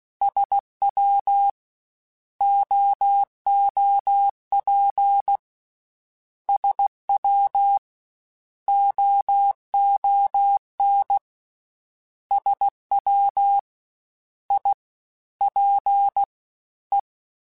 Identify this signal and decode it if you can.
Morse code, heard as a keyed tone, about 16 wpm, spaced as for 11 wpm, 800 Hz. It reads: SWOOP SWOON SWIPE
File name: Misc / Misc-mp3s / Jumps @15